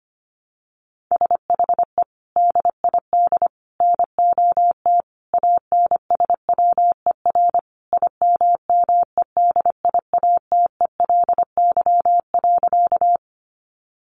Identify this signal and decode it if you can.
Morse code, heard as a keyed tone, about 25 wpm, 700 Hz. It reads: SHE DID NOT ANSWER IMMEDIATELY.
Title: Sentences from Top 1100 Words 25wpm